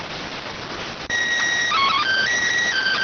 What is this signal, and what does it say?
SSTV (slow-scan television), PD 290.